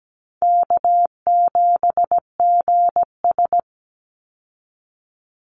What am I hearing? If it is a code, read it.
Morse code, heard as a keyed tone, about 17 wpm, 700 Hz. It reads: K7GS
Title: Call Signs Repeated 3x - 17wpm